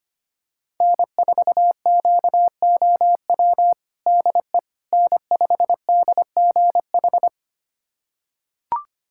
Morse code, heard as a keyed tone, about 25 wpm, 700 Hz. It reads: N4QOW DE N5DGH
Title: USA Callsign DE Callsign - 5 Characters - 25wpm